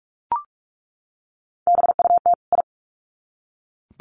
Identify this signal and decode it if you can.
Morse code, heard as a keyed tone, about 44 wpm, 700 Hz. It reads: BUT I
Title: Sets of 2 Words 45wpm